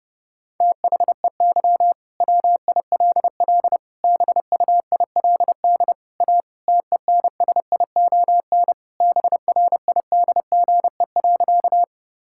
Morse code, heard as a keyed tone, about 30 wpm, 700 Hz. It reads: THEY WILL BUILD A TENSION BRIDGE.
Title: Sentences from Top 1600 Words 30wpm